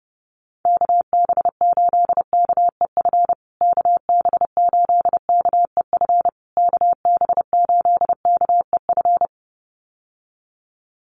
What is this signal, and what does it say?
Morse code, heard as a keyed tone, about 30 wpm, 700 Hz. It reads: KB8KEF KB8KEF KB8KEF
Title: USA Call Signs Repeated 3x 30wpm